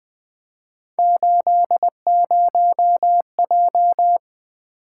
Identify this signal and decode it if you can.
Morse code, heard as a keyed tone, about 20 wpm, 700 Hz. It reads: 80J
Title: Three Letter-Number Combinations - 20wpm